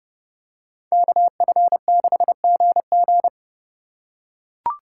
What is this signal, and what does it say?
Morse code, heard as a keyed tone, about 29 wpm, 700 Hz. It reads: KF6GG